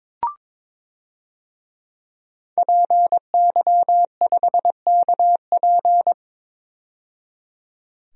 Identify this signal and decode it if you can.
Morse code, heard as a keyed tone, about 22 wpm, 700 Hz. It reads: PY5KP
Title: DXCC-100 Call Signs - 5 Characters - 22wpm